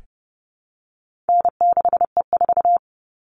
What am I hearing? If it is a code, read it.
Morse code, heard as a keyed tone, about 30 wpm, 700 Hz. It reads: N6E4
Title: All Cut Numbers - 4-Digit 30wpm